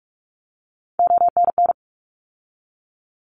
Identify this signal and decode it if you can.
Morse code, heard as a keyed tone, about 44 wpm, 700 Hz. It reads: ONN
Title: ARRL Sections 45wpm